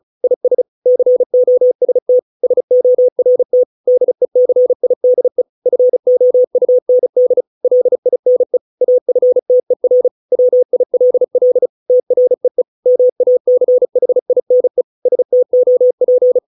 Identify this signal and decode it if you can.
Morse code, heard as a keyed tone, about 34 wpm, 500 Hz. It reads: IS COST SORT DECIDE FOUND LINE AFTER WILL TREE MACHINE STOP